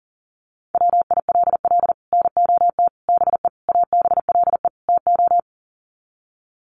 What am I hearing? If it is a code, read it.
Morse code, heard as a keyed tone, about 40 wpm, 700 Hz. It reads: WILL NOT BE ABLE TO
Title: Sets of 5 Words 40wpm